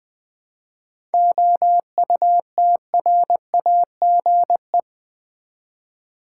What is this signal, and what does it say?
Morse code, heard as a keyed tone, about 20 wpm, 700 Hz. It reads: OUTRAGE
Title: 7 Letter Words - First 3 Letters Given - 20wpm